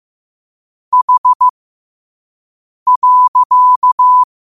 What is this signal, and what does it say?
Morse code, heard as a keyed tone, about 15 wpm, spaced as for 5 wpm, 1000 Hz. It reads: H.